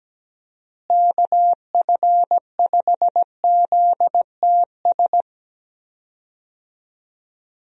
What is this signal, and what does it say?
Morse code, heard as a keyed tone, about 17 wpm, 700 Hz. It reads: KF5ZTS
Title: USA Call Signs - 6 Characters - Repeated 3x - 17wpm